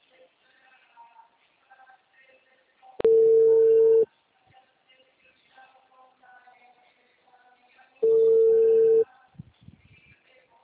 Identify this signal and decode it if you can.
Ringback tone (China, 450 Hz).
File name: out- -2040- - - .1736